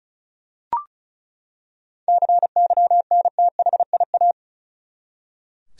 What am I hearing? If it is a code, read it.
Morse code, heard as a keyed tone, about 36 wpm, 700 Hz. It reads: CYNTHIA
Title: USA Common Names - 35wpm